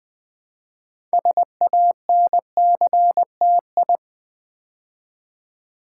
Morse code, heard as a keyed tone, about 20 wpm, 700 Hz. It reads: SANCTI
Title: Build A Word - 2 Letters At A Time - 8-Letter Words - 20wpm